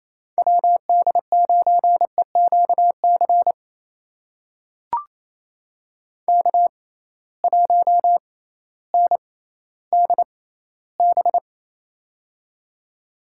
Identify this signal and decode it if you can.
Morse code, heard as a keyed tone, about 28 wpm, spaced as for 15 wpm, 700 Hz. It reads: WD9EQC K1NDB
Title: USA Call Signs - 6x Character Spacing - 28wpm